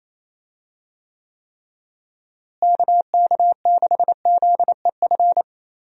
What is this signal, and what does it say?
Morse code, heard as a keyed tone, about 28 wpm, 700 Hz. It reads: KK6ZEF
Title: USA Call Signs - 6 Characters - Repeated 3x - 28wpm